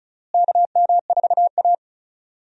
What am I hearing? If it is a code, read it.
Morse code, heard as a keyed tone, about 34 wpm, 700 Hz. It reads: KM4A
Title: USA Call Signs - 4 Characters - Repeated 2x - 35wpm